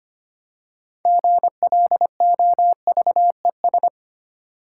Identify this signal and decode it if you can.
Morse code, heard as a keyed tone, about 30 wpm, 700 Hz. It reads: GLOVES
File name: Easy Words - Speed-Racing with Hints - 25wpm